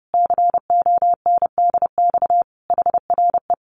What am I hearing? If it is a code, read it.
Morse code, heard as a keyed tone, about 30 wpm, 700 Hz. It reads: CONDX HRE
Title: Common QSO Phrases 30wpm